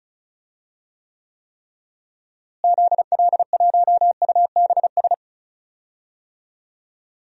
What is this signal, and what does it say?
Morse code, heard as a keyed tone, about 36 wpm, 700 Hz. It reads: ZL1UBS
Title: DXCC-100 Call Signs - 6 Characters - Repeated 3x - 35wpm